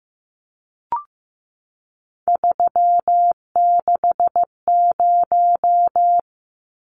Morse code, heard as a keyed tone, about 15 wpm, 700 Hz. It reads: 360
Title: Numbers - 3-Digit 15wpm